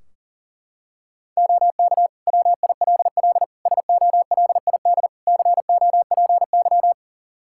Morse code, heard as a keyed tone, about 40 wpm, 700 Hz. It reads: OK WILL SOLID COPY
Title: Practice QSO - Exchange 3 - More Info - 40wpm